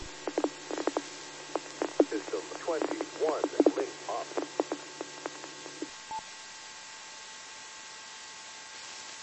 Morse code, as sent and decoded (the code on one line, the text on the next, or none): none